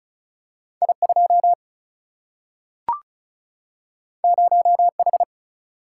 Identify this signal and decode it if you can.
Morse code, heard as a keyed tone, about 36 wpm, 700 Hz. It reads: I2 0H
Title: Single-Letter Single-Number 35wpm